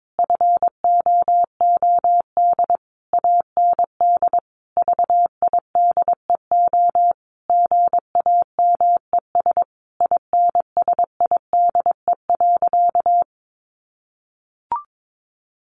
Morse code, heard as a keyed tone, about 22 wpm, 700 Hz. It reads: FOOD AND VIDEO GAMES INSIDE.
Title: Easy Phrases 22wpm